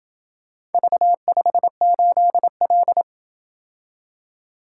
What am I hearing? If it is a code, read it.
Morse code, heard as a keyed tone, about 27 wpm, 700 Hz. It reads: V58L